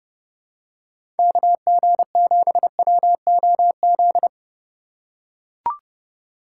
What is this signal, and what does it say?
Morse code, heard as a keyed tone, about 30 wpm, 700 Hz. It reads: KG7WOZ
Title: USA Call Signs - 6 Characters - 2x Character Spacing - 30wpm